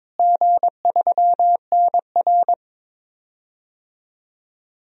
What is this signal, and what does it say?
Morse code, heard as a keyed tone, about 22 wpm, 700 Hz. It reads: G3NR
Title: DXCC-100 Call Signs - 4 Characters - Repeated 3x - 22wpm